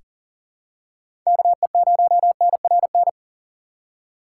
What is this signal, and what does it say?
Morse code, heard as a keyed tone, about 37 wpm, 700 Hz. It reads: KE0NRN
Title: USA Call Signs - 6 Characters - 4x Character Spacing - 40wpm